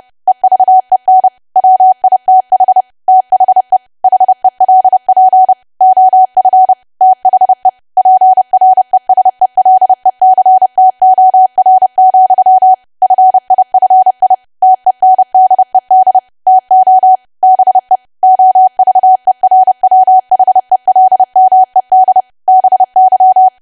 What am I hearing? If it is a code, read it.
Morse code, heard as a keyed tone, about 30 wpm, 750 Hz. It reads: EVEN WITH THE HELP OF THE PRESELECTOR, FIFI TENDED TO BE OVERWHELMED BY
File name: cwa / W1AW-Level3 / 30WPM